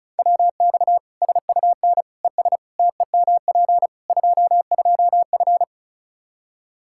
Morse code, heard as a keyed tone, about 36 wpm, 700 Hz. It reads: WX SUN ES TEMP 22F
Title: Practice QSO - Exchange 2 - Info - 35wpm